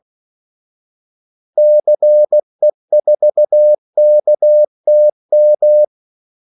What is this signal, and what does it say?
Morse code, heard as a keyed tone, about 16 wpm, 600 Hz. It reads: CE4KTM
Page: Kaldesignaler 16 wpm